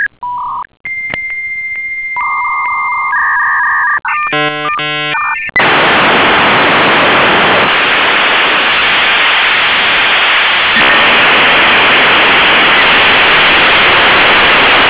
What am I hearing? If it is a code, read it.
Dial-up modem handshake.